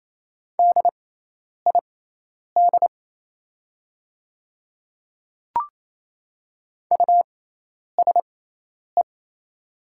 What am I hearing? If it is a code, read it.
Morse code, heard as a keyed tone, about 28 wpm, spaced as for 10 wpm, 700 Hz. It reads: DID USE
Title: 6x Character Spacing - Top 100 Words - 28wpm